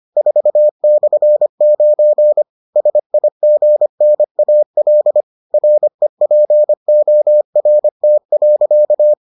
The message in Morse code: 4/9 SIGNAL REPORT.

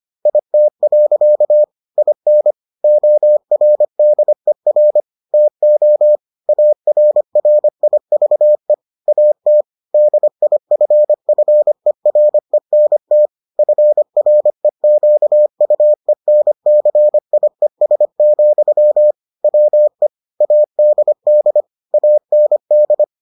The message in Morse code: IT. IN ORDER TO ARRIVE AT DIFFERENT FREQUENCIES, WE ADD AND

